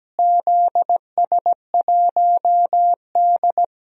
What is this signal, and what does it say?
Morse code, heard as a keyed tone, about 17 wpm, 700 Hz. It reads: ZS1D
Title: DXCC-100 Call Signs - 4 Characters - Repeated 2x - 17wpm